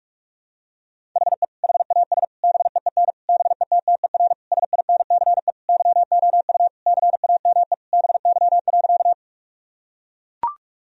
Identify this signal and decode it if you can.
Morse code, heard as a keyed tone, about 43 wpm, 700 Hz. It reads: HE HAS BEEN BETTER SINCE YOU CAME BY.
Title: Sentences from Top 1000 Words - Review Part 1 - 45wpm